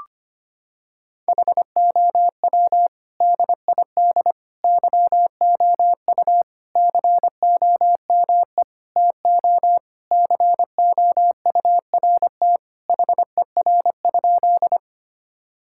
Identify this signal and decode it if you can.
Morse code, heard as a keyed tone, about 25 wpm, 700 Hz. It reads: HOW DID YOU COME TO COURT HER?
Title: Sentences from Top 1600 Words 25wpm